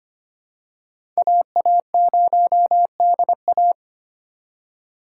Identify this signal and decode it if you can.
Morse code, heard as a keyed tone, about 25 wpm, 700 Hz. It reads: AA0DA
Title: USA Call Signs - 5 Characters - 2x Character Spacing - 25wpm